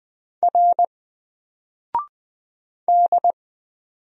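Morse code, heard as a keyed tone, about 20 wpm, 700 Hz. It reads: R D
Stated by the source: Single letter and number Morse drill at 20 wpm, from N4LCD.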